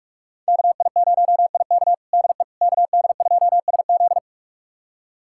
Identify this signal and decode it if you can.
Morse code, heard as a keyed tone, about 45 wpm, 700 Hz. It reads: KI0IK DE KD2SZ
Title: USA Callsign DE Callsign - 5 Characters - 45wpm